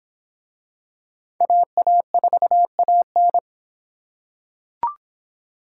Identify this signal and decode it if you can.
Morse code, heard as a keyed tone, about 25 wpm, 700 Hz. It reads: AA4AN